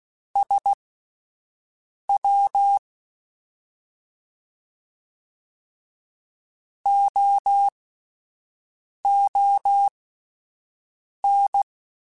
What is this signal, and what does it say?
Morse code, heard as a keyed tone, about 16 wpm, spaced as for 5 wpm, 800 Hz. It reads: SW OON
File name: Misc / Misc-mp3s / Jumps @5